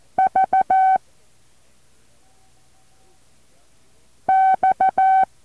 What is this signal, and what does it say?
Morse code, heard as a keyed tone, about 14 wpm, 760 Hz. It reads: V X
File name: intl morse code2